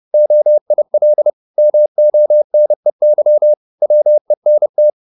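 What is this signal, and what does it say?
Morse code, heard as a keyed tone, about 30 wpm, 600 Hz. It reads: OIL MONEY WENT
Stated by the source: Morse code at 30 wpm, the novel Never by Ken Follett in English, from CW med Gnister.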